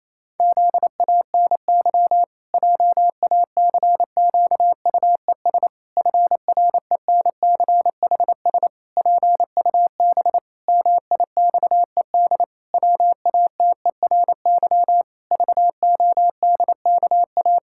Morse code, heard as a keyed tone, about 28 wpm, 700 Hz. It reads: ZANY JACQUES FRENCHS PUB MIXED WATERY VODKA
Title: Pangram Challenge v2 - 28wpm